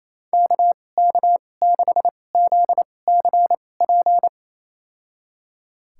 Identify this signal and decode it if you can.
Morse code, heard as a keyed tone, about 28 wpm, spaced as for 20 wpm, 700 Hz. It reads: KK6ZCP
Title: USA Call Signs - 6 Characters - 2x Character Spacing - 28wpm